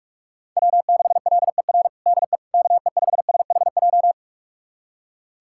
Morse code, heard as a keyed tone, about 41 wpm, 700 Hz. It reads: W6LER DE KE5SHJ
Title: USA Callsign DE Callsign 45wpm